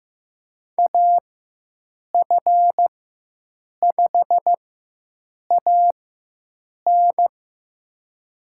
Morse code, heard as a keyed tone, about 15 wpm, spaced as for 7 wpm, 700 Hz. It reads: AF5AN
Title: USA Call Signs - 5 Characters - 4x Character Spacing - 15wpm